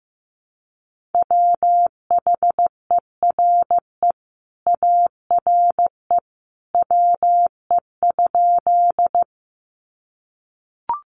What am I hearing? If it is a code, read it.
Morse code, heard as a keyed tone, about 15 wpm, 700 Hz. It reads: WHERE ARE WE?
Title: Sentences from Top 200 Words 15wpm